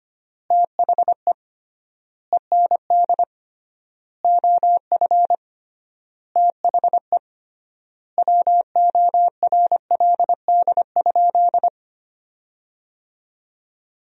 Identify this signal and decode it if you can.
Morse code, heard as a keyed tone, about 25 wpm, 700 Hz. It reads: THE END OF THE WORLD?
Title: 3x Word-Spacing - Sentences from Top 200 Words - 25wpm